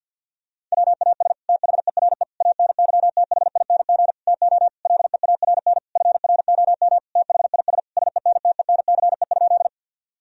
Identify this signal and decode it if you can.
Morse code, heard as a keyed tone, about 52 wpm, 700 Hz. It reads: WAS THERE ANYTHING TO LEARN FROM THIS SENTENCE?
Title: Sentences from Top 1000 Words - Review Part 1 - 50wpm